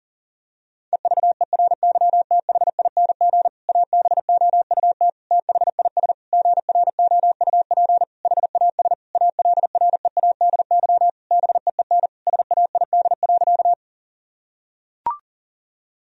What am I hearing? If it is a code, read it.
Morse code, heard as a keyed tone, about 40 wpm, 700 Hz. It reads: EVERYTHING ABOUT THIS GROUP HAS ALREADY BEEN SAID.
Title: Sentences from Top 1000 Words - Review Part 1 - 40wpm